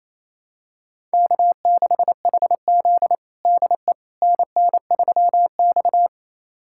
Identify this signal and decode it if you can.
Morse code, heard as a keyed tone, about 27 wpm, 700 Hz. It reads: K6HZ DE NN3X
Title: USA Callsign DE Callsign - 4 Characters - 28wpm